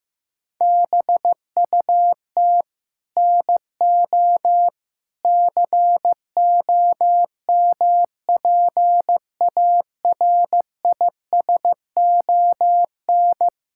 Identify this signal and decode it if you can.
Morse code, heard as a keyed tone, about 15 wpm, 700 Hz. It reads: BUT NO COMPARISON
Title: Sentences from Top 1600 Words 15wpm